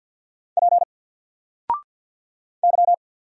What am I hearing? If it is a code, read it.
Morse code, heard as a keyed tone, about 50 wpm, 700 Hz. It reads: P Y
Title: Single Letter-Number 50wpm